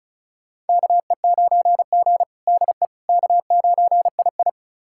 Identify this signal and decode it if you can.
Morse code, heard as a keyed tone, about 34 wpm, 700 Hz. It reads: KE9G DE K9II
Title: USA Callsign DE Callsign - 4 Characters - 35wpm